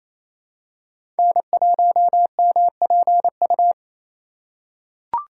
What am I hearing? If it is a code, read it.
Morse code, heard as a keyed tone, about 28 wpm, 700 Hz. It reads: N1MPU
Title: USA Call Signs - 2x Character Spacing - 28wpm